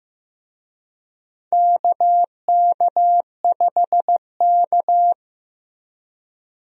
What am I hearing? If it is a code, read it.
Morse code, heard as a keyed tone, about 15 wpm, 700 Hz. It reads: KK5K
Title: USA Call Signs - 4 Characters - Repeated 2x - 15wpm